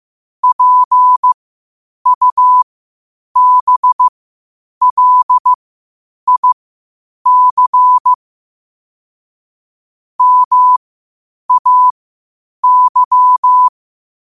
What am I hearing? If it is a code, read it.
Morse code, heard as a keyed tone, about 15 wpm, spaced as for 8 wpm, 1000 Hz. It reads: PUBLIC MAY
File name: qotdfiles / 8wpm